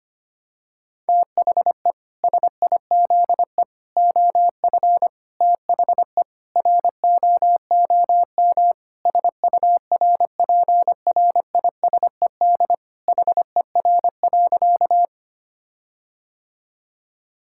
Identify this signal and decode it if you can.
Morse code, heard as a keyed tone, about 25 wpm, 700 Hz. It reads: THE SIZE OF THE ROOM SURPRISED HER.
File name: practice / 25wpm / Sentences with 4-Word Phrases - 25wpm